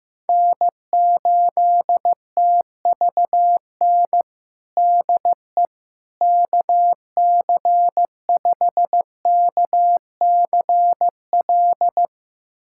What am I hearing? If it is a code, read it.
Morse code, heard as a keyed tone, about 15 wpm, 700 Hz. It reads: N8TVN DE KC5KCL
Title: USA Callsign DE Callsign 15wpm